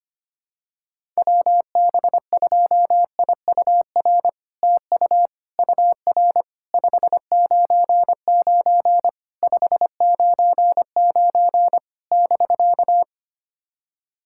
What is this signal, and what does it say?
Morse code, heard as a keyed tone, about 25 wpm, 700 Hz. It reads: WB2IUR TU UR 599 599 <BK>
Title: SOTA QSO Practice 25wpm